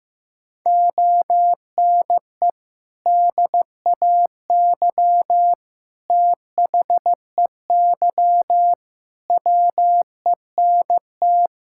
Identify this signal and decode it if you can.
Morse code, heard as a keyed tone, about 16 wpm, 700 Hz. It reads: ONE DAY THEY WENT
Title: Sentences from Top 1600 Words 15wpm